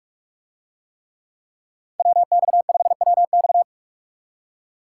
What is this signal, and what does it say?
Morse code, heard as a keyed tone, about 44 wpm, 700 Hz. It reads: WX5WX